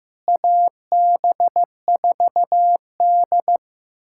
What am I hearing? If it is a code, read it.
Morse code, heard as a keyed tone, about 15 wpm, 700 Hz. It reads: AB4D
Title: USA Callsign DE Callsign - 4 Characters - 15wpm